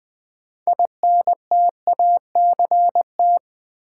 Morse code, heard as a keyed tone, about 20 wpm, 700 Hz. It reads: INTACT
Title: Build A Word - 2 Letters At A Time - 6-Letter Words - 20wpm